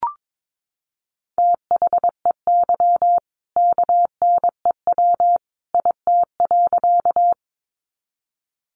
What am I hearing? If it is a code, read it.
Morse code, heard as a keyed tone, about 22 wpm, 700 Hz. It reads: THEY KNEW IT.
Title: Sentences from Top 400 Words 22wpm